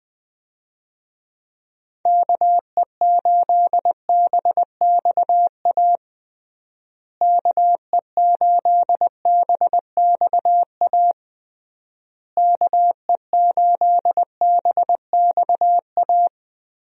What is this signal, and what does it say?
Morse code, heard as a keyed tone, about 20 wpm, 700 Hz. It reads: KE8BXA KE8BXA KE8BXA
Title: USA Call Signs - 6 Characters - Repeated 3x - 20wpm